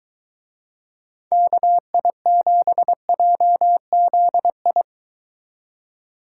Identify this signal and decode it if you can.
Morse code, heard as a keyed tone, about 23 wpm, 700 Hz. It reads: KI7JZI